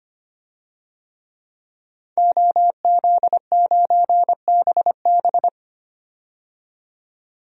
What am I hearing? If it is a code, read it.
Morse code, heard as a keyed tone, about 25 wpm, 700 Hz. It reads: OZ9BB
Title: DXCC-100 Call Signs - 5 Characters - 25wpm